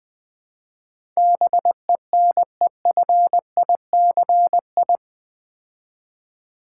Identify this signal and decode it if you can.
Morse code, heard as a keyed tone, about 20 wpm, 700 Hz. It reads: BENEFICI
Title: Build A Word - 4 Letters At A Time - 10-Letter Words - 20wpm